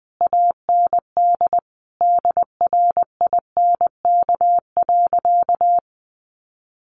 Morse code, heard as a keyed tone, about 20 wpm, 700 Hz. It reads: AND DRINK.
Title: Sentences from Top 1500 Words 20wpm